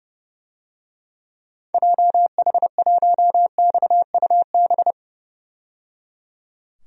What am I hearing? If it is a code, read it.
Morse code, heard as a keyed tone, about 30 wpm, 700 Hz. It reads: JH1XUB
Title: DXCC-100 Call Signs - 6 Characters - 30wpm